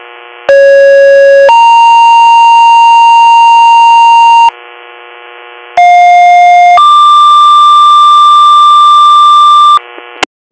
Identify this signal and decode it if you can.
Two-tone page, heard as a one-second tone, then a three-second tone, about 550 then 930 Hz, about 710 then 1180 Hz. It is heard from 0.5 to 9.8 s.